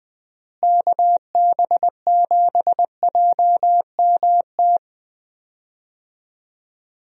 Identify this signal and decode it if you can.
Morse code, heard as a keyed tone, about 20 wpm, 700 Hz. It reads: KB7JMT
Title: USA Call Signs - 6 Characters - Repeated 3x - 20wpm